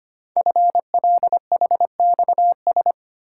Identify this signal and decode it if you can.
Morse code, heard as a keyed tone, about 25 wpm, 700 Hz. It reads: FLHXS
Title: Code Groups - 5 Letters - 25wpm